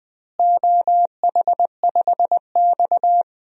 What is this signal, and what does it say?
Morse code, heard as a keyed tone, about 21 wpm, 700 Hz. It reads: OH5X
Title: DXCC-100 Call Signs - 4 Characters - Repeated 2x - 20wpm